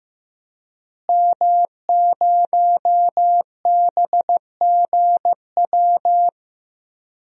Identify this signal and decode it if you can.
Morse code, heard as a keyed tone, about 15 wpm, 700 Hz. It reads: M0BGW
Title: DXCC-100 Call Signs - 5 Characters - Repeated 2x - 15wpm